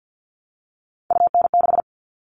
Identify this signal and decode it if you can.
Morse code, heard as a keyed tone, about 50 wpm, 700 Hz. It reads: UN6
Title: All Cut Numbers - 3-Digit 50wpm